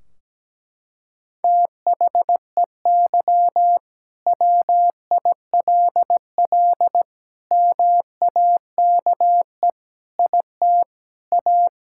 Morse code, heard as a keyed tone, about 17 wpm, 700 Hz. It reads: THEY WILL MAKE IT A
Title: Sentences from Top 1600 Words 17wpm